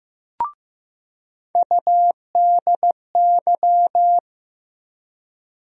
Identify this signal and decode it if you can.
Morse code, heard as a keyed tone, about 15 wpm, 700 Hz. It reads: UDY
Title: USA Call Sign Postfixes - 3 Letters 15wpm